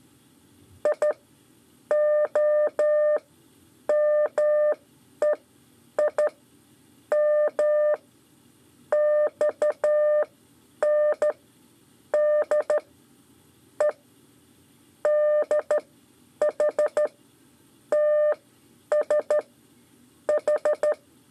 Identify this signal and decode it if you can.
Morse code, heard as a keyed tone, about 12 wpm, spaced as for 8 wpm, 600 Hz. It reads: IOMEI M XN D E DH TS H
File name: 5 LEZIONE RIPASSO 2